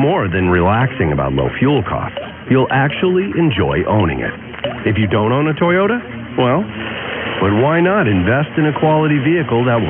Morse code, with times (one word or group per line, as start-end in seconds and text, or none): none